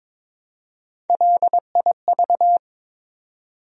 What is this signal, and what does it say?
Morse code, heard as a keyed tone, about 22 wpm, 700 Hz. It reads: LIV
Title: USA Common Names - Max 3 Letters - 22wpm